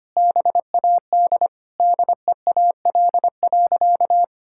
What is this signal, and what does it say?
Morse code, heard as a keyed tone, about 25 wpm, 700 Hz. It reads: BAD DEAL.
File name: practice / 25wpm / Sentences with 9-Word Phrases - 25wpm